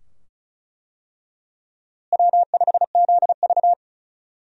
Morse code, heard as a keyed tone, about 34 wpm, 700 Hz. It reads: W5ZV
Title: USA Call Signs - 4 Characters - Repeated 2x - 35wpm